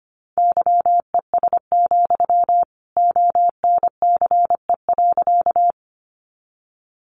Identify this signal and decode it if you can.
Morse code, heard as a keyed tone, about 25 wpm, 700 Hz. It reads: YES, ONCE.
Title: Sentences from Top 500 Words 25wpm